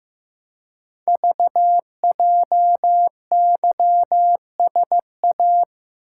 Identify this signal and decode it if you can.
Morse code, heard as a keyed tone, about 15 wpm, 700 Hz. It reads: VJYSA
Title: Code Groups - 5 Letters - 15wpm